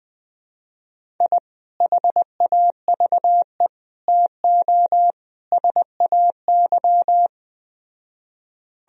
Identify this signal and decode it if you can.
Morse code, heard as a keyed tone, about 20 wpm, 700 Hz. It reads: I HAVE TO SAY
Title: Speed-Racing - Sets of 4 Words 17wpm